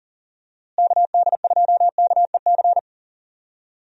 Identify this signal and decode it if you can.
Morse code, heard as a keyed tone, about 40 wpm, 700 Hz. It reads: KD2KEC
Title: USA Call Signs - 6 Characters - Repeated 2x - 40wpm